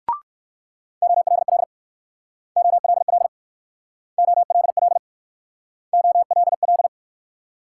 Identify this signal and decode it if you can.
Morse code, heard as a keyed tone, about 63 wpm, 690 Hz. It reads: OLL OLL OLL OLL